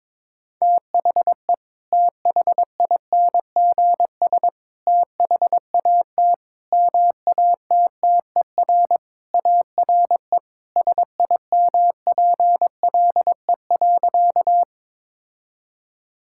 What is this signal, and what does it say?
Morse code, heard as a keyed tone, about 22 wpm, 700 Hz. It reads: THE THINGS THAT MATTER ARE SIMPLE.
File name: practice / 22wpm / Sentences with 3-Word Phrases - 22wpm